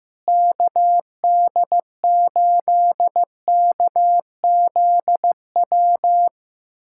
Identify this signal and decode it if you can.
Morse code, heard as a keyed tone, about 15 wpm, 700 Hz. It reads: KD8KZW
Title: USA Callsign DE Callsign 15wpm